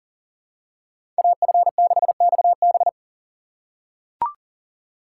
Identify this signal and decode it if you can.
Morse code, heard as a keyed tone, about 40 wpm, 700 Hz. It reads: AF6XB
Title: USA Call Signs - 5 Characters - 2x Character Spacing - 40wpm